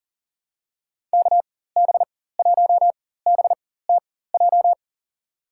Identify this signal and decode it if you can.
Morse code, heard as a keyed tone, about 41 wpm, spaced as for 19 wpm, 700 Hz. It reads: KB1BTJ
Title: USA Call Signs - 6 Characters - 4x Character Spacing - 40wpm